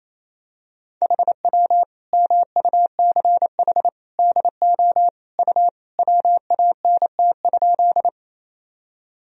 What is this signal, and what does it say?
Morse code, heard as a keyed tone, about 28 wpm, 700 Hz. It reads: HW MUCH DO U WANT?
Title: Sentences from Top 200 Words and Abbreviations 28wpm